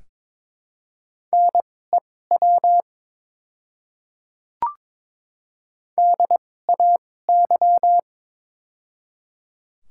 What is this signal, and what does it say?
Morse code, heard as a keyed tone, about 22 wpm, spaced as for 16 wpm, 700 Hz. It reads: NEW DAY
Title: 2x Character Spacing - Top 300 Words - 22wpm